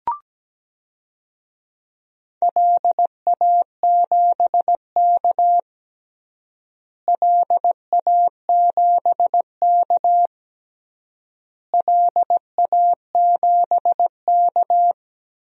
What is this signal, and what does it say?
Morse code, heard as a keyed tone, about 17 wpm, 700 Hz. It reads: LA7K LA7K LA7K
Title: DXCC-100 Call Signs - 4 Characters - Repeated 3x - 17wpm